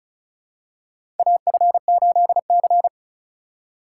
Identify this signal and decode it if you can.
Morse code, heard as a keyed tone, about 34 wpm, 700 Hz. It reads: AF8C
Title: USA Call Signs - 3x Character Spacing - 35wpm